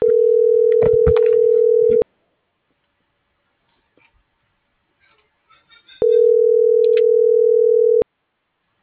Ringback tone (North America), 440 + 480 Hz.